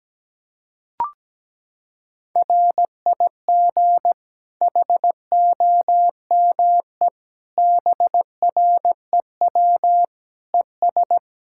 Morse code, heard as a keyed tone, about 17 wpm, 700 Hz. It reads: RIG HOME BREW ES
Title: Practice QSO - Exchange 2 - Info - 17wpm